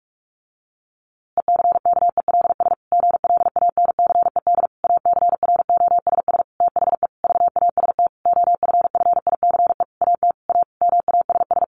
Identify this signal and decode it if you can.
Morse code, heard as a keyed tone, about 44 wpm, 700 Hz. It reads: ECKELS GLANCED ACROSS THE VAST OFFICE AT A MASS
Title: A Sound Of Thunder 45wpm